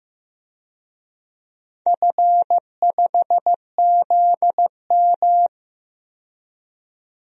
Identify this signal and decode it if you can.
Morse code, heard as a keyed tone, about 15 wpm, 700 Hz. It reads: F5ZM
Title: DXCC-100 Call Signs - 4 Characters - 15wpm